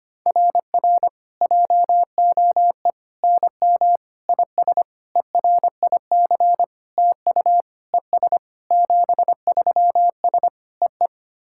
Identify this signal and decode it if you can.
Morse code, heard as a keyed tone, about 25 wpm, 700 Hz. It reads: RR JOE NM IS ERIC TU ES 73S EE
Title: SOTA QSO Practice 25wpm